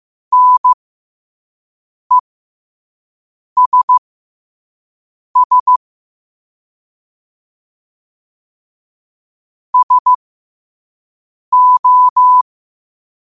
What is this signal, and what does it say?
Morse code, heard as a keyed tone, about 15 wpm, spaced as for 5 wpm, 1000 Hz. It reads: NESS SO
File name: 5wpm